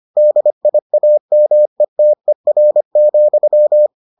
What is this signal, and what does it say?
Morse code, heard as a keyed tone, about 25 wpm, 600 Hz. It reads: DIAMETER,